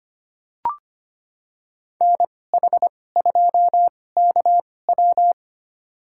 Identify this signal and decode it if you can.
Morse code, heard as a keyed tone, about 25 wpm, spaced as for 18 wpm, 700 Hz. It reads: NH2KW
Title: USA Call Signs - 2x Character Spacing - 25wpm